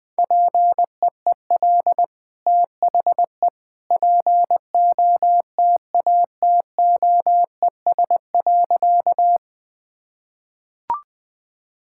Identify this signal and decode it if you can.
Morse code, heard as a keyed tone, about 20 wpm, 700 Hz. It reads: PEEL THE POTATOES.
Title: Easy Sentences with Hints - 20wpm